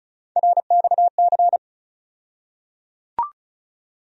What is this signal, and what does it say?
Morse code, heard as a keyed tone, about 34 wpm, 700 Hz. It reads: RXC
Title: USA Call Sign Postfixes - 3 Letters 35wpm